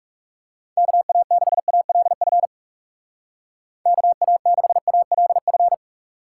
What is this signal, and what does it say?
Morse code, heard as a keyed tone, about 36 wpm, 700 Hz. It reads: KA6ALF KA6ALF
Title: Speed Racing - USA 2x3 Call Signs 30wpm